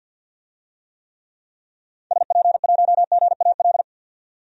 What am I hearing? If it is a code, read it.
Morse code, heard as a keyed tone, about 47 wpm, 700 Hz. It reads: SP1GAL
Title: DXCC-100 Call Signs - 6 Characters - 50wpm